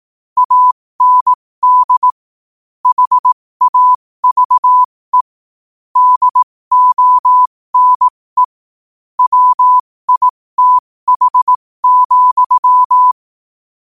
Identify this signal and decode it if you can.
Morse code, heard as a keyed tone, about 18 wpm, spaced as for 15 wpm, 1000 Hz. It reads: AND HAVE DONE WITH,
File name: qotdfiles / 15wpm